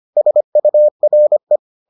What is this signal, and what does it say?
Morse code, heard as a keyed tone, about 25 wpm, 600 Hz. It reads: SURE